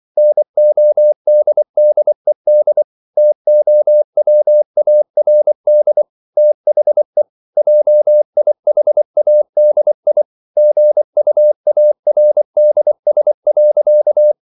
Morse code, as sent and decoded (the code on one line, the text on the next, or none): -. --- -.. -.. -.. / - --- .-- .- .-. -.. / - .... / .--- .. .... .- -.. .. / --. ..- .- .-. -.. ... .-.-.-
NODDED TOWARD THE JIHADI GUARDS.